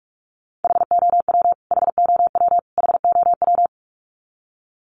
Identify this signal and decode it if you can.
Morse code, heard as a keyed tone, about 44 wpm, 700 Hz. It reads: HOW HOW HOW
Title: Top 200 Words Repeated 45wpm